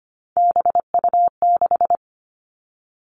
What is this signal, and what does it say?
Morse code, heard as a keyed tone, about 25 wpm, 700 Hz. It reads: BU6
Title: Three Challenging Letter-Number Combinations 25wpm